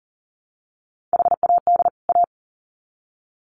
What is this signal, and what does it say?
Morse code, heard as a keyed tone, about 40 wpm, 700 Hz. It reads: HAD A
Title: Sets of 2 Words 40wpm